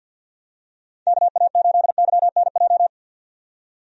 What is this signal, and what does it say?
Morse code, heard as a keyed tone, about 50 wpm, 700 Hz. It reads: KA8YNJ
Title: USA Call Signs - 6 Characters - 50wpm